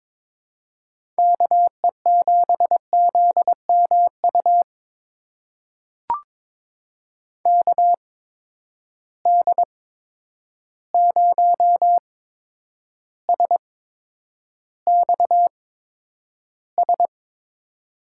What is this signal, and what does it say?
Morse code, heard as a keyed tone, about 22 wpm, spaced as for 9 wpm, 700 Hz. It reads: KE7ZMU KD0SXS
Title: USA Call Signs - 6 Characters - 8x Character Spacing - 22wpm